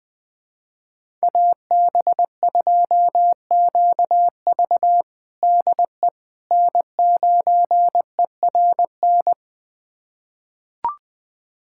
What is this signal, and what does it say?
Morse code, heard as a keyed tone, about 20 wpm, 700 Hz. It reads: AB2QV DE N9ERN
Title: USA Callsign DE Callsign - 5 Characters - 20wpm